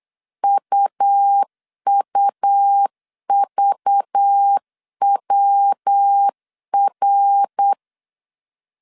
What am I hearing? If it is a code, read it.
Morse code, heard as a keyed tone, about 8 wpm, 800 Hz. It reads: UUVWR
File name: CW / DET / 1 9 1500C S B 8wpm 5624